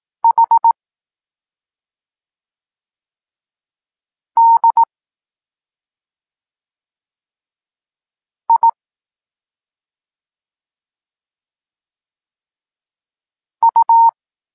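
Morse code, heard as a keyed tone, about 18 wpm, 920 Hz. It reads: H D I U
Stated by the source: Evenly spaced Morse lesson text 21 for learning letters G and Z, from SP9MOA Niepołomicki Klub Krótkofalowców.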